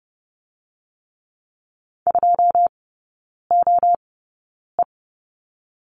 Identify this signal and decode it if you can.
Morse code, heard as a keyed tone, about 30 wpm, 700 Hz. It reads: 2 O E
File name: 081 Lesson 14e - Number 2 - Three Characters 30wpm